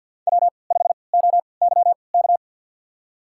Morse code, heard as a keyed tone, about 52 wpm, spaced as for 29 wpm, 700 Hz. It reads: W5OYK
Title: USA Call Signs - 5 Characters - 3x Character Spacing - 50wpm